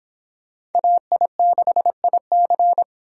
Morse code, heard as a keyed tone, about 26 wpm, 700 Hz. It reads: AI6IC